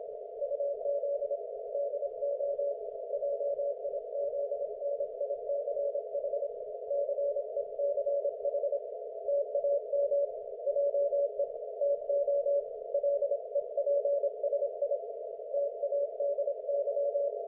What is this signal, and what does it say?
Morse code, heard as a keyed tone, about 26 wpm, 570 Hz. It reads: UZ DO JIZNICH CECH TAM JE TO LEPSI TADY
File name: Archiv / CUC216 / ok1fhi cul216